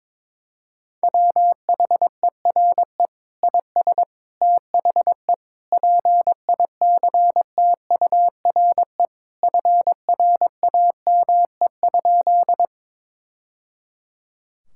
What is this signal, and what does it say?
Morse code, heard as a keyed tone, about 22 wpm, 700 Hz. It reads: WHERE IS THE PICTURE FRAME?
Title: Sentences from Top 1000 Words - Review Part 6 - 22wpm